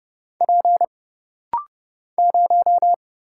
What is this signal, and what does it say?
Morse code, heard as a keyed tone, about 30 wpm, 700 Hz. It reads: P 0
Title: Single Letter-Number - Rapid-Fire 30wpm